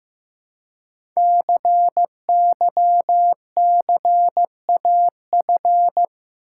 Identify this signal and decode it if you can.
Morse code, heard as a keyed tone, about 15 wpm, 700 Hz. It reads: CYCAF
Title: Code Groups - 5 Letters - 15wpm